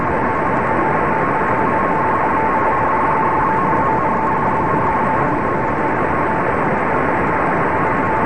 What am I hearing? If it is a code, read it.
Two-tone teleprinter signal, about 200 Hz shift.